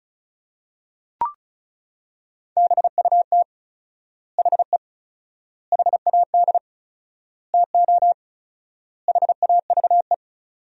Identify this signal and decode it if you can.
Morse code, heard as a keyed tone, about 34 wpm, 700 Hz. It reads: BUT HE HAD TO HAVE
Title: 4x Word-Spacing - Sentences from Top 100 Words - 35wpm